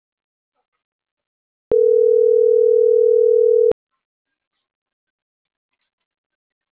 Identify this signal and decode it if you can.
Ringback tone (North America), 440 + 480 Hz.